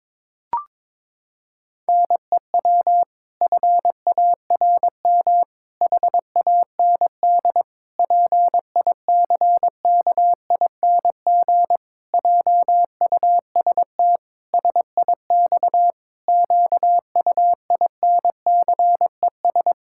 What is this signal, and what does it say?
Morse code, heard as a keyed tone, about 22 wpm, 700 Hz. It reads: NEW FARM HAND PICKING JUST SIX QUINCES
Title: Pangram Challenge v2 - 22wpm